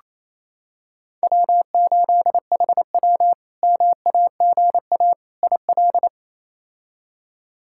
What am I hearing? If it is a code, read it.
Morse code, heard as a keyed tone, about 28 wpm, 700 Hz. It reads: W8HW MAGA IL
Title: K1USN SST QSO Practice - 28wpm